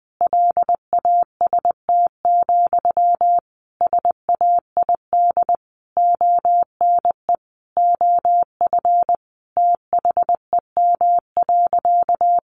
Morse code, heard as a keyed tone, about 20 wpm, 700 Hz. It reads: LAST, SAID ONE OF THEM.
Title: Sentences from Top 300 Words 20wpm